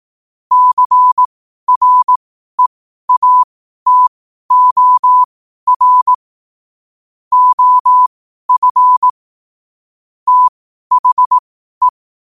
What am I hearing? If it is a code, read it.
Morse code, heard as a keyed tone, about 18 wpm, spaced as for 12 wpm, 1000 Hz. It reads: CREATOR OF THE
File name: qotdfiles / 12wpm